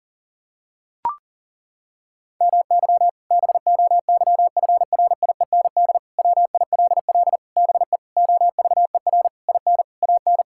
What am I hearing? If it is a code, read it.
Morse code, heard as a keyed tone, about 37 wpm, 700 Hz. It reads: MY BOYFRIEND WILL BE OVER IN AN
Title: Sentences from Top 1000 Words - Review Part 5 - 40wpm